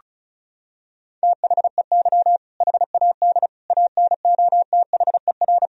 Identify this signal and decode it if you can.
Morse code, heard as a keyed tone, about 34 wpm, 700 Hz. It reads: THEY HAD ANOTHER
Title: Sentences from Top 1000 Words - Review Part 1 - 35wpm